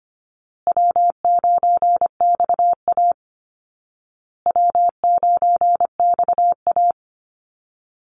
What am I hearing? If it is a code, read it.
Morse code, heard as a keyed tone, about 25 wpm, 700 Hz. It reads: W9XA W9XA
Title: Easy Call Signs Repeated 3x - 25wpm